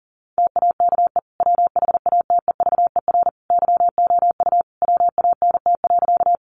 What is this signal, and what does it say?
Morse code, heard as a keyed tone, about 40 wpm, 700 Hz. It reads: TAKE WHATEVER YOU WANT.
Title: Sentences from Top 1100 Words 40wpm